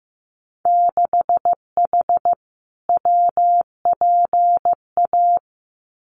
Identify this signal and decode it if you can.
Morse code, heard as a keyed tone, about 15 wpm, 700 Hz. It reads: 6H WPA
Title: Winter Field Day Exchanges Encore 15wpm